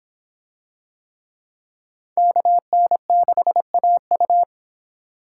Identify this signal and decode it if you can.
Morse code, heard as a keyed tone, about 24 wpm, 700 Hz. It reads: KN6AU